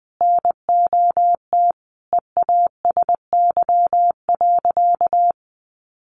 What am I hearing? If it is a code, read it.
Morse code, heard as a keyed tone, about 20 wpm, 700 Hz. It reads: NOT EASY.
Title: Sentences from Top 1500 Words 20wpm